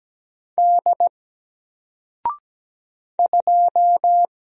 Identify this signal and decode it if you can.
Morse code, heard as a keyed tone, about 17 wpm, 700 Hz. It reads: D 2
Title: Single Letter-Number 17wpm